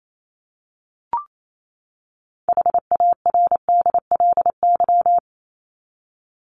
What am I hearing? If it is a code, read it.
Morse code, heard as a keyed tone, about 28 wpm, 700 Hz. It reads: HARDLY
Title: LY Suffix - Top 100 Words with Base Words 28wpm